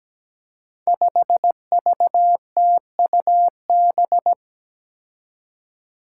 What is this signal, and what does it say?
Morse code, heard as a keyed tone, about 17 wpm, 700 Hz. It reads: 5VTUB
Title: Five Challenging Letter-Number Combinations 17wpm